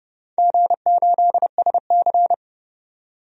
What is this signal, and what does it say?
Morse code, heard as a keyed tone, about 31 wpm, 700 Hz. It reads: G8SC
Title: DXCC-100 Call Signs - 4 Characters - 30wpm